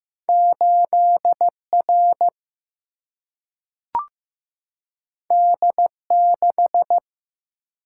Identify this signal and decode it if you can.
Morse code, heard as a keyed tone, about 15 wpm, 700 Hz. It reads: 8 R D 6
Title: Single-Letter Single-Number 15wpm